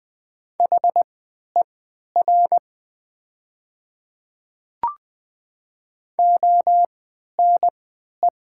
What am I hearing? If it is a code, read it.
Morse code, heard as a keyed tone, about 20 wpm, spaced as for 11 wpm, 700 Hz. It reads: HER ONE
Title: 3x Character Spacing - Top 100 Words - 20wpm